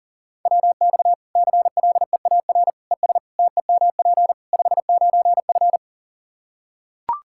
Morse code, heard as a keyed tone, about 40 wpm, 700 Hz. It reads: WX CLEAR ES TEMP 59F
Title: K1USN SST QSO Practice 40wpm